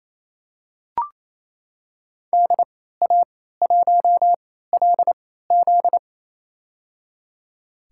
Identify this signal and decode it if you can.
Morse code, heard as a keyed tone, about 28 wpm, spaced as for 16 wpm, 700 Hz. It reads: DA1LZ
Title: USA Call Signs - 3x Character Spacing - 28wpm